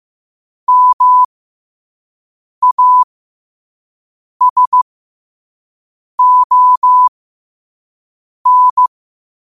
Morse code, heard as a keyed tone, about 15 wpm, spaced as for 5 wpm, 1000 Hz. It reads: MASON
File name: qotdfiles / 5wpm